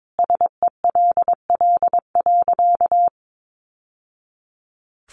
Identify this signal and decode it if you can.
Morse code, heard as a keyed tone, about 22 wpm, 700 Hz. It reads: SELL.
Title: Sentences from Top 1100 Words 22wpm